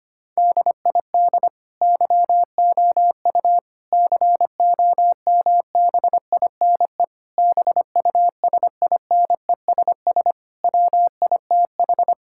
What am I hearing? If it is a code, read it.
Morse code, heard as a keyed tone, about 25 wpm, 700 Hz. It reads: DID YOU COMBINE BUSINESS WITH
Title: Sentences from Top 1600 Words 25wpm